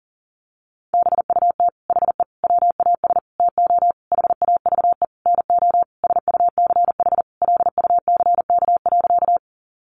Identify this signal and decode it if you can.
Morse code, heard as a keyed tone, about 40 wpm, 700 Hz. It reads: BUT HE WAS TO HAVE NO SUCH LUCK.
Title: Sentences from Top 1500 Words 40wpm